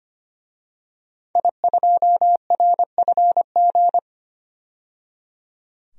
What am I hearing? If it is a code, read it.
Morse code, heard as a keyed tone, about 25 wpm, 700 Hz. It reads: I2RFG
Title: DXCC-100 Call Signs - 5 Characters - 25wpm